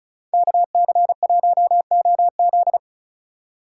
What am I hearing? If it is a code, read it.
Morse code, heard as a keyed tone, about 43 wpm, 700 Hz. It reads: KC1OZ
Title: Speed Racing - USA 2x2 Call Signs 35wpm